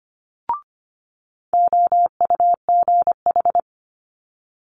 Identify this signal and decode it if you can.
Morse code, heard as a keyed tone, about 25 wpm, 700 Hz. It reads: OUGH
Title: Multi-letter Phonemes Quick Fire 25wpm